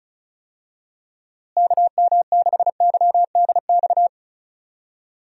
Morse code, heard as a keyed tone, about 36 wpm, 700 Hz. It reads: KM6YDX
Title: USA Call Signs - 6 Characters - Repeated 3x - 35wpm